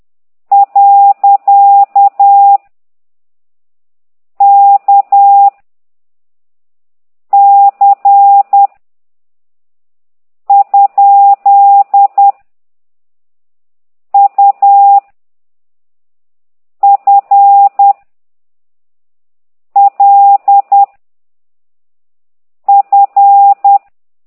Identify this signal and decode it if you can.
Morse code, heard as a keyed tone, about 10 wpm, 800 Hz. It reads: . K C ? U F L F
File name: ct2idw / exerc9